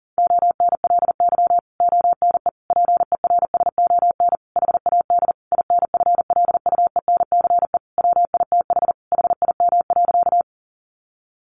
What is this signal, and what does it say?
Morse code, heard as a keyed tone, about 37 wpm, 700 Hz. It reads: ONLY ONE PERSON HAD INFLUENCE WITH HIM.
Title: Sentences from Top 1100 Words 40wpm